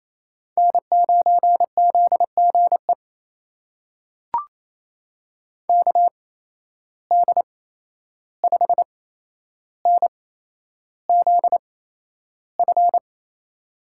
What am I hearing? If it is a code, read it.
Morse code, heard as a keyed tone, about 28 wpm, spaced as for 11 wpm, 700 Hz. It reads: N9ZGE KD5NZF
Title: USA Call Signs - 8x Character Spacing - 28wpm